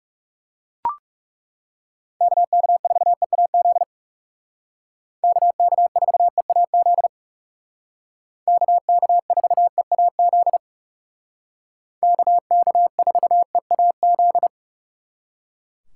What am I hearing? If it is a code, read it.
Morse code, heard as a keyed tone, about 36 wpm, 700 Hz. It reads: KK4EAZ KK4EAZ KK4EAZ KK4EAZ
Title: Speed Racing - USA Call Signs 30wpm